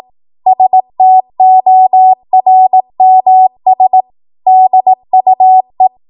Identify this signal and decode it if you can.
Morse code, heard as a keyed tone, about 18 wpm, 750 Hz. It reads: STORMS DUE
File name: Morse / 18WPM